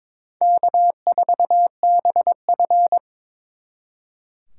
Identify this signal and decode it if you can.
Morse code, heard as a keyed tone, about 26 wpm, 700 Hz. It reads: K4BF
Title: Speed Racing - USA 1x2 Call Signs 22wpm